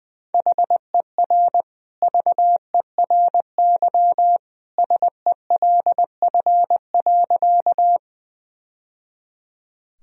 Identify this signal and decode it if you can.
Morse code, heard as a keyed tone, about 21 wpm, 700 Hz. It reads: HER VERY SELF.
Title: Sentences from Top 1000 Words - Review Part 1 - 20wpm